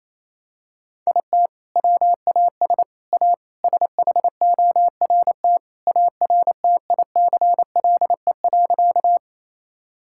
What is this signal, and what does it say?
Morse code, heard as a keyed tone, about 28 wpm, 700 Hz. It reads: IT WAS A SHORT ARTICLE.
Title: Sentences from Top 1000 Words - Review Part 5 - 28wpm